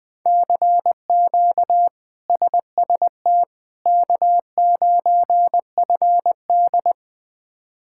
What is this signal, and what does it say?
Morse code, heard as a keyed tone, about 20 wpm, 700 Hz. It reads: CQ SST K9FD
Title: K1USN SST QSO Practice - 20wpm